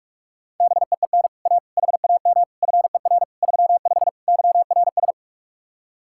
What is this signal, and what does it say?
Morse code, heard as a keyed tone, about 48 wpm, 700 Hz. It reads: BEEN A HAM FER 35 YRS